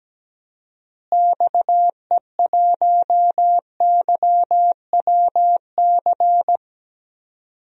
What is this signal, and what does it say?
Morse code, heard as a keyed tone, about 17 wpm, 700 Hz. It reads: XE1YWC
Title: DXCC-100 Call Signs - 6 Characters - Repeated 3x - 17wpm